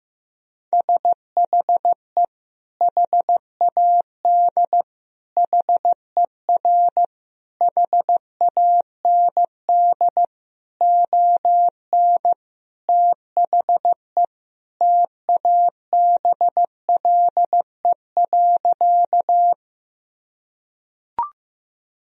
Morse code, heard as a keyed tone, about 16 wpm, 700 Hz. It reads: SHE HAD HER HAND ON THE TABLE.
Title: Sentences from Top 1000 Words - Review Part 2 - 15wpm